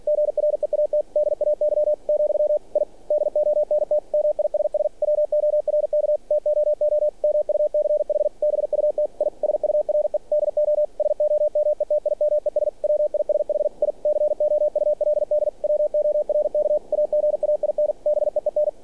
Morse code, heard as a keyed tone, about 46 wpm, 600 Hz. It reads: GREAT DAY , I DONT MISS WORK TOO MUCH BUT I SURE DO SOMETIMES WISH I COULD WORK AGAIN BEEN
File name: archived / CRSnet